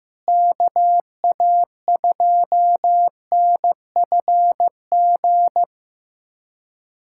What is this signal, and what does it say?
Morse code, heard as a keyed tone, about 16 wpm, 700 Hz. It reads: KA2NFG
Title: USA Callsign DE Callsign - 6 Characters - 15wpm